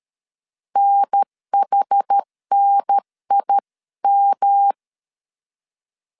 Morse code, heard as a keyed tone, about 13 wpm, 800 Hz. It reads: NHNIM
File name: CW / DET / 1 3 1500C S A 12wpm 2905